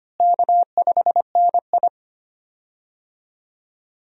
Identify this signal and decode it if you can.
Morse code, heard as a keyed tone, about 25 wpm, 700 Hz. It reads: K5NI
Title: USA Call Signs - 4 Characters - Repeated 3x - 25wpm